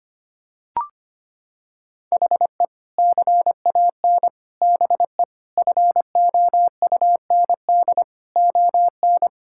Morse code, heard as a keyed tone, about 25 wpm, 700 Hz. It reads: HE CAN BE FOUND ON
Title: Sentences from Top 1600 Words 25wpm